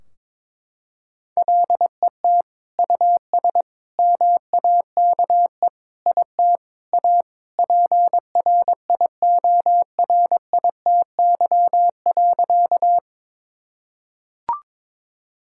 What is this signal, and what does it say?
Morse code, heard as a keyed tone, about 22 wpm, 700 Hz. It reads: LET US MAKE IT A PRIORITY.
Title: Sentences from Top 1600 Words 22wpm